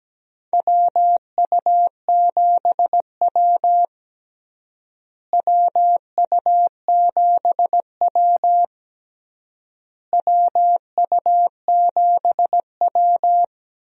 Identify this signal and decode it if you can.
Morse code, heard as a keyed tone, about 17 wpm, 700 Hz. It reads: WU7W WU7W WU7W
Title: USA Call Signs - 4 Characters - Repeated 3x - 17wpm